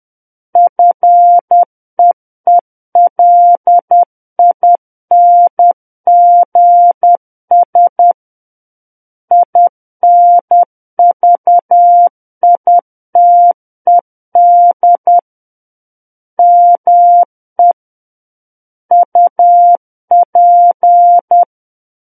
Morse code, heard as a keyed tone, about 10 wpm, 700 Hz. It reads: FEELINGS INVITED ME UP
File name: Book One / 01-Chapter 1 - 10 WPM